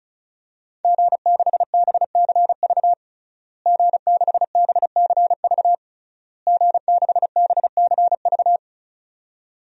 Morse code, heard as a keyed tone, about 35 wpm, 700 Hz. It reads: G6BCV G6BCV G6BCV
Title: DXCC-100 Call Signs - 5 Characters - Repeated 3x - 35wpm